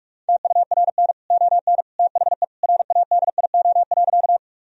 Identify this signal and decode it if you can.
Morse code, heard as a keyed tone, about 46 wpm, 700 Hz. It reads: TURN ON THE RADIO.
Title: Sentences from Top 1000 Words - Review Part 3 - 45wpm